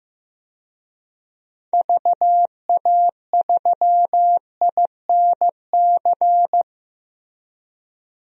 Morse code, heard as a keyed tone, about 15 wpm, 700 Hz. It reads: VA3INC
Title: DXCC-100 Call Signs - 6 Characters - 15wpm